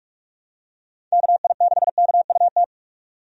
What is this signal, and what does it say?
Morse code, heard as a keyed tone, about 46 wpm, 700 Hz. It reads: KI6KUT
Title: USA Call Signs - 6 Characters - 45wpm